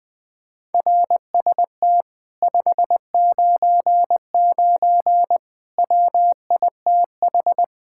Morse code, heard as a keyed tone, about 20 wpm, 700 Hz. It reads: RST 599 WITH
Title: Speed-Racing Common QSO Phrases - 17wpm